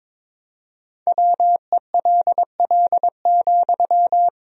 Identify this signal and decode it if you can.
Morse code, heard as a keyed tone, about 22 wpm, 700 Hz. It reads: WELL,
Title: Sentences from Top 200 Words and Abbreviations 22wpm